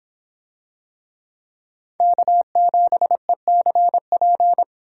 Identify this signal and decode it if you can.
Morse code, heard as a keyed tone, about 23 wpm, 700 Hz. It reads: K7ECP